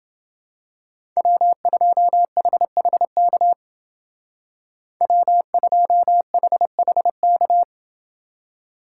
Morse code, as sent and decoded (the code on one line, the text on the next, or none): .-- ..--- .... .... -.- / .-- ..--- .... .... -.-
W2HHK W2HHK